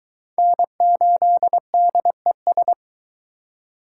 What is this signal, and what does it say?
Morse code, heard as a keyed tone, about 23 wpm, 700 Hz. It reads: N8DES